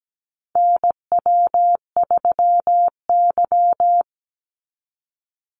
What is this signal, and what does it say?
Morse code, heard as a keyed tone, about 17 wpm, 700 Hz. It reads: NW3Y
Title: Call Signs Repeated 2x - 17wpm